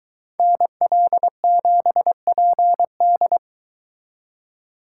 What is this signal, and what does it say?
Morse code, heard as a keyed tone, about 22 wpm, 700 Hz. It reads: NL7PD